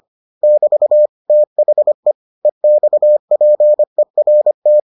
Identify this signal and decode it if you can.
Morse code, heard as a keyed tone, about 25 wpm, 600 Hz. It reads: <BT> THE EXPERT